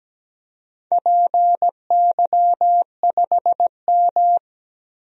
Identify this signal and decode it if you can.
Morse code, heard as a keyed tone, about 17 wpm, 700 Hz. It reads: PY5M
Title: DXCC-100 Call Signs - 4 Characters - 17wpm